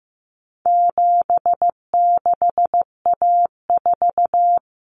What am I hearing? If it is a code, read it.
Morse code, heard as a keyed tone, about 15 wpm, 700 Hz. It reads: 76A4
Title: Common Cut Numbers - 4-Digit 15wpm